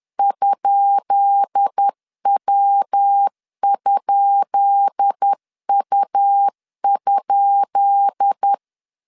Morse code, heard as a keyed tone, about 10 wpm, 800 Hz. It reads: ?W?U?
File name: LEZIONI / 10 SIGNS / 10B INT 10wpm 2844